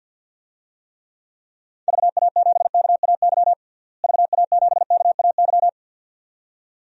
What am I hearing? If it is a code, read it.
Morse code, heard as a keyed tone, about 52 wpm, 700 Hz. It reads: VA7KAY VA7KAY
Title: DXCC-100 Call Signs - 6 Characters - Repeated 2x - 50wpm